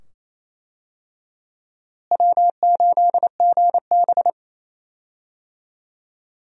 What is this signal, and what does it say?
Morse code, heard as a keyed tone, about 27 wpm, 700 Hz. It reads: W8GB
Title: USA Call Signs - 4 Characters - Repeated 2x - 28wpm